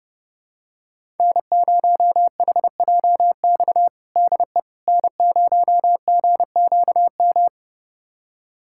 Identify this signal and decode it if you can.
Morse code, heard as a keyed tone, about 32 wpm, 700 Hz. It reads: N0HJX DE N0GQM
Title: USA Callsign DE Callsign - 5 Characters - 30wpm